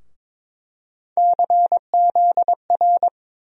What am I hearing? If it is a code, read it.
Morse code, heard as a keyed tone, about 22 wpm, 700 Hz. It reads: CZR
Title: USA Call Sign Postfixes - 3 Letters 22wpm